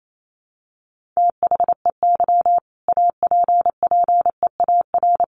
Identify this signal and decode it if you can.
Morse code, heard as a keyed tone, about 28 wpm, 700 Hz. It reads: THEY APPEAR
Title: Sentences from Top 1400 Words 28wpm